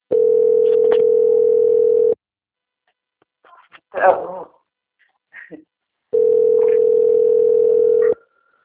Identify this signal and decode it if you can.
Ringback tone (North America), 440 + 480 Hz, with speech.
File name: out- - - - - .7091